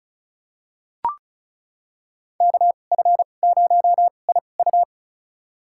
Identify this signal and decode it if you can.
Morse code, heard as a keyed tone, about 36 wpm, spaced as for 25 wpm, 700 Hz. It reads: KF0IU
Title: USA Call Signs - 5 Characters - 2x Character Spacing - 35wpm